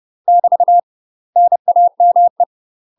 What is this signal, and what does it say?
Morse code, heard as a keyed tone, about 30 wpm, 700 Hz. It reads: <BT> NAME
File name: UK MASTER QSOs TRAINING / 30WPM / UK MASTER QSOs 0038